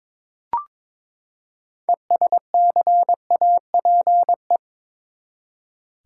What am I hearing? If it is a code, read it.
Morse code, heard as a keyed tone, about 22 wpm, 700 Hz. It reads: ESCAPE
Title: Six Letter Words - 22wpm